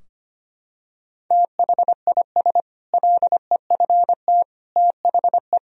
Morse code, heard as a keyed tone, about 25 wpm, 700 Hz. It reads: THIS LEFT THE
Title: Sentences from Top 1000 Words - Review Part 4 - 25wpm